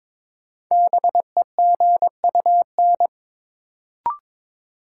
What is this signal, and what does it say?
Morse code, heard as a keyed tone, about 22 wpm, 700 Hz. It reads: BEGUN
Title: Sentences from Top 1000 Words - Review Part 6 - New Word Drill - 22wpm